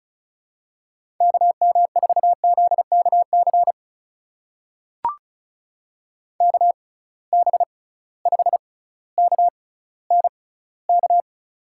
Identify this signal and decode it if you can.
Morse code, heard as a keyed tone, about 36 wpm, spaced as for 18 wpm, 690 Hz. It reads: KM4ZKC KB5KNK
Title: USA Call Signs - 6 Characters - 6x Character Spacing - 35wpm